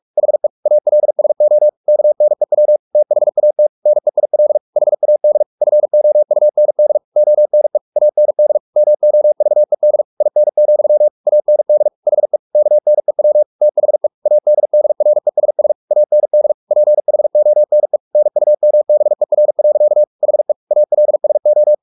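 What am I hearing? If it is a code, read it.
Morse code, heard as a keyed tone, about 44 wpm, 600 Hz. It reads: HE ALSO KNEW THAT NEIL HAD FOUND ONE AND MOVED IN, AND HE KNEW THE ADDRESS AND PHONE NUMBER. HE ALSO
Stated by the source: Morse practice at 45 wpm, the novel Never by Ken Follett in English, from CW med Gnister.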